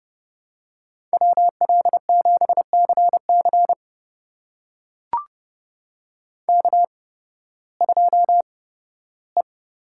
Morse code, heard as a keyed tone, about 30 wpm, spaced as for 16 wpm, 700 Hz. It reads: WL7CC K2E
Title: USA Call Signs - 8x Character Spacing - 30wpm